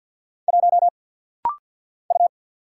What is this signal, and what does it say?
Morse code, heard as a keyed tone, about 49 wpm, 700 Hz. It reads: 1 U
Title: Single Letter-Number Alt - Rapid-Fire 50wpm